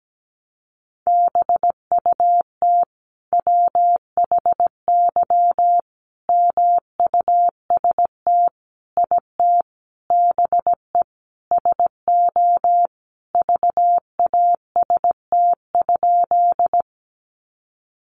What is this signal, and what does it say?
Morse code, heard as a keyed tone, about 17 wpm, 700 Hz. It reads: BUT WHY MUST IT BE SO VAST?
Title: Sentences from Top 1300 Words 17wpm